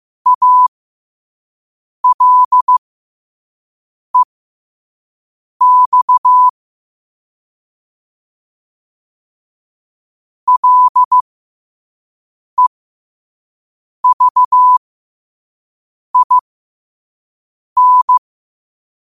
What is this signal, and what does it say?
Morse code, heard as a keyed tone, about 15 wpm, spaced as for 5 wpm, 1000 Hz. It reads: ALEX LEVIN
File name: qotdfiles / 5wpm